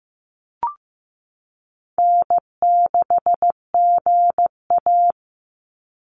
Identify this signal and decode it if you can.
Morse code, heard as a keyed tone, about 15 wpm, 700 Hz. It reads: N6GA
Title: Easy Call Signs Repeated 2x - 15wpm